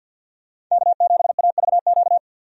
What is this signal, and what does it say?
Morse code, heard as a keyed tone, about 50 wpm, 700 Hz. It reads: K7AVQ
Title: USA Call Signs - 5 Characters - 50wpm